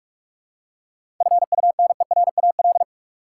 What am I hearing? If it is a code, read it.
Morse code, heard as a keyed tone, about 46 wpm, 700 Hz. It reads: FUNERAL
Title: Seven Letter Words - 45wpm